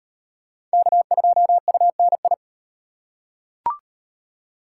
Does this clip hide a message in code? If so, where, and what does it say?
Yes: Morse code from 0.7 to 2.3 s, heard as a keyed tone, about 36 wpm, 700 Hz. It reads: K2UNI